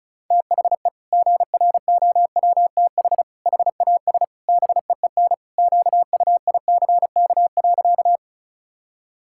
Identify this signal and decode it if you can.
Morse code, heard as a keyed tone, about 34 wpm, 700 Hz. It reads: THE GROWTH HAS BEEN QUICK.
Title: Sentences from Top 1000 Words - Review Part 5 - 35wpm